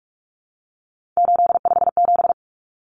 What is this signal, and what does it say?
Morse code, heard as a keyed tone, about 44 wpm, 700 Hz. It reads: 857
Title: Numbers - 3-Digit 45wpm